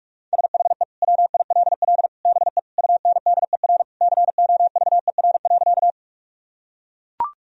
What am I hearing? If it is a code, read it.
Morse code, heard as a keyed tone, about 42 wpm, 700 Hz. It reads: SHE WILL BE UNDER COVER.
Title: Sentences from Top 1000 Words - Review Part 1 - 45wpm